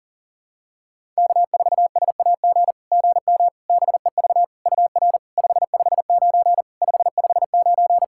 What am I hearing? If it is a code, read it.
Morse code, heard as a keyed tone, about 40 wpm, 700 Hz. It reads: K4SAG GM BEV UR 559 559
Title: SOTA QSO Practice 40wpm